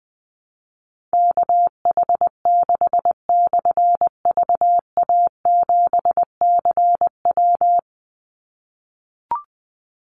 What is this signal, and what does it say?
Morse code, heard as a keyed tone, about 20 wpm, 700 Hz. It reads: KH6/VA7CW
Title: Call Signs - 20wpm